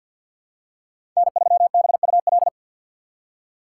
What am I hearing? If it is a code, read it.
Morse code, heard as a keyed tone, about 52 wpm, 690 Hz. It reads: N3BUL
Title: USA Call Signs - 2x Character Spacing - 50wpm